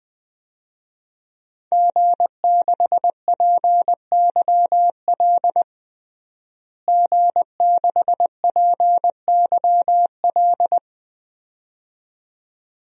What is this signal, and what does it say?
Morse code, heard as a keyed tone, about 20 wpm, 700 Hz. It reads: G6PYL G6PYL
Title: DXCC-100 Call Signs - 5 Characters - Repeated 2x - 20wpm